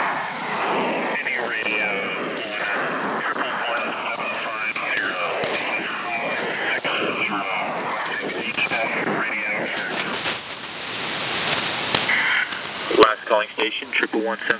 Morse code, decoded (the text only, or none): none